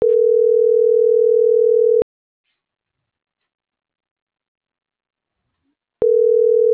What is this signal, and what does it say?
Ringback tone (North America), 440 + 480 Hz.